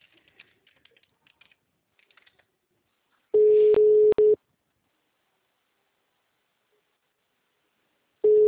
Ringback tone (Europe, 425 Hz).
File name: out- -2012- - - .46469